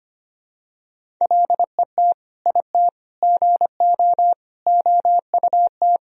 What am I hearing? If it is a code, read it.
Morse code, heard as a keyed tone, about 25 wpm, 700 Hz. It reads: LET IT GO OUT
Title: Sentences from Top 1000 Words - Review Part 1 - 25wpm